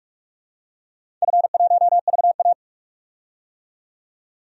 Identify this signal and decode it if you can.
Morse code, heard as a keyed tone, about 42 wpm, 700 Hz. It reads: F1VA
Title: DXCC-100 Call Signs - 4 Characters - 45wpm